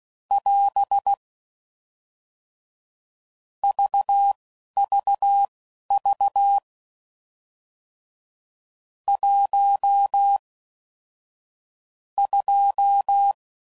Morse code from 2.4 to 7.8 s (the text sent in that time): VVV